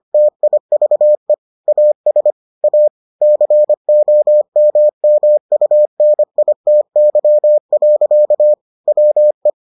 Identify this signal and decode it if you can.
Morse code, heard as a keyed tone, about 25 wpm, 600 Hz. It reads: TIVE AS A COMMUNITY. WE